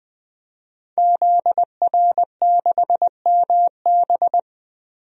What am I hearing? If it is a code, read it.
Morse code, heard as a keyed tone, about 20 wpm, 700 Hz. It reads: ZR6MB
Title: DXCC-100 Call Signs - 5 Characters - 20wpm